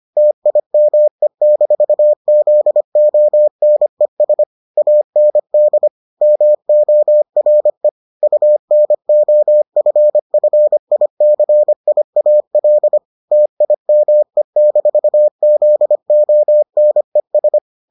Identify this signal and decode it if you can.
Morse code, heard as a keyed tone, about 25 wpm, 600 Hz. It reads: TIME-ZONES AND MORE UNOFFICIAL TIME-ZONES